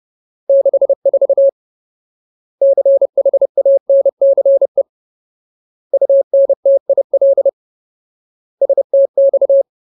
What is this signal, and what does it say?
Morse code, heard as a keyed tone, about 30 wpm, 550 Hz. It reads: 64 CHANCE UNTIL STX
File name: Ultimate Word List / 30WPM Ultimate